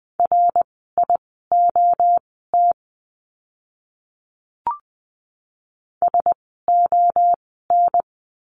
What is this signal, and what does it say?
Morse code, heard as a keyed tone, about 20 wpm, spaced as for 14 wpm, 700 Hz. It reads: RIOT SON
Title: 061 Lesson 11f - Letter h - Words 20wpm